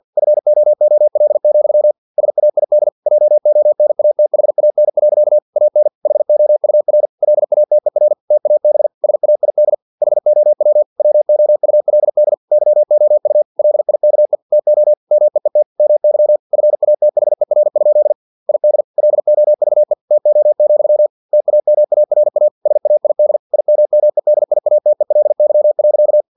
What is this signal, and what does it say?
Morse code, heard as a keyed tone, about 50 wpm, 600 Hz. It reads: FOOL, SAID JONATHAN. AN HOUR LATER TAB SAID HOW WOULD YOU LIKE TO MEET MY FATHER? ID LOVE TO, TAMARA SAID IMMEDIATELY.
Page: Never 49wpm